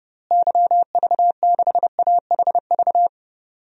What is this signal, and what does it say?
Morse code, heard as a keyed tone, about 30 wpm, 700 Hz. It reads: YV6AHV
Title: DXCC-100 Call Signs - 6 Characters - Repeated 2x - 30wpm